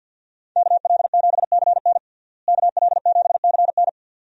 Morse code, heard as a keyed tone, about 48 wpm, 700 Hz. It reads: KL7CN KL7CN
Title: USA Call Signs - 5 Characters - Repeated 2x - 50wpm